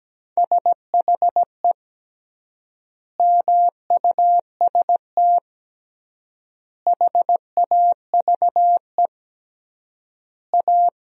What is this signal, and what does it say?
Morse code, heard as a keyed tone, about 17 wpm, 700 Hz. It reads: SHE MUST HAVE A
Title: 3x Word-Spacing - Sentences from Top 300 Words - 17wpm